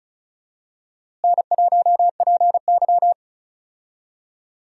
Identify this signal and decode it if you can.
Morse code, heard as a keyed tone, about 36 wpm, 700 Hz. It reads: N1PY
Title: USA Call Signs - 4 Characters - Repeated 2x - 35wpm